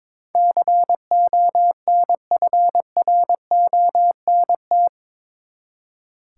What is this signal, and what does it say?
Morse code, heard as a keyed tone, about 22 wpm, 700 Hz. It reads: CONFRONT
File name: practice / 22wpm / Eight Letter Words - 22wpm